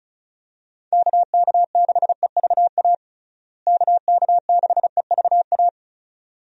Morse code, heard as a keyed tone, about 36 wpm, 700 Hz. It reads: KK6EVA KK6EVA
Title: USA Call Signs - 6 Characters - Repeated 2x - 35wpm